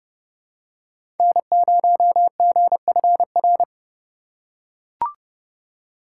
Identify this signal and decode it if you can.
Morse code, heard as a keyed tone, about 32 wpm, 700 Hz. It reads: N0GFR
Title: USA Call Signs - 5 Characters - 4x Character Spacing - 30wpm